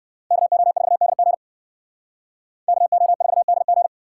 Morse code, heard as a keyed tone, about 58 wpm, 690 Hz. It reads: KO4DG KO4DG